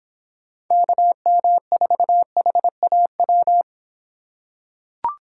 Morse code, heard as a keyed tone, about 25 wpm, 700 Hz. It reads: KM4HAW